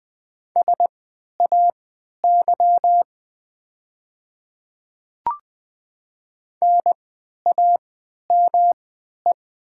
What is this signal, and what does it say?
Morse code, heard as a keyed tone, about 20 wpm, spaced as for 11 wpm, 700 Hz. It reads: SAY NAME
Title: 3x Character Spacing - Top 300 Words - 20wpm